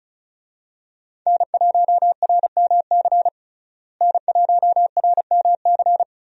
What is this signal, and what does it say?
Morse code, heard as a keyed tone, about 36 wpm, 700 Hz. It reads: N1RMC N1RMC
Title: USA Call Signs - 5 Characters - Repeated 3x - 35wpm